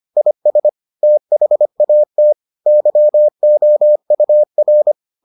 Morse code, heard as a keyed tone, about 25 wpm, 600 Hz. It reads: IS THAT YOUR